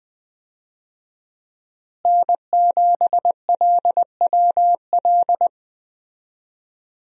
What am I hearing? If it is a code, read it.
Morse code, heard as a keyed tone, about 20 wpm, 700 Hz. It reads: N7LWL
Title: USA Call Signs - 5 Characters - Repeated 3x - 20wpm